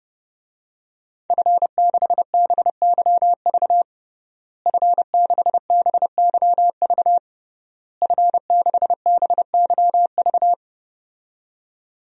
Morse code, heard as a keyed tone, about 30 wpm, 700 Hz. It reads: F6BYV F6BYV F6BYV
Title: DXCC-100 Call Signs - 5 Characters - Repeated 3x - 30wpm